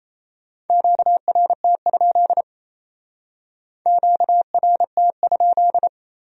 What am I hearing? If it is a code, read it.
Morse code, heard as a keyed tone, about 35 wpm, 700 Hz. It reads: QRT? QRT?
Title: Speed-Racing Q-Codes - 28wpm